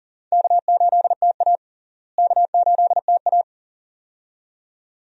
Morse code, heard as a keyed tone, about 38 wpm, 700 Hz. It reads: K8TA K8TA
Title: USA Call Signs - 4 Characters - Repeated 2x - 40wpm